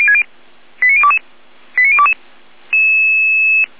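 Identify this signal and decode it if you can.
Five-tone selective call (ZVEI), heard as quick runs of five tones, tones of about 80 ms. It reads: 89022 89022
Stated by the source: Five-tone ZVEI selective call received on SDR.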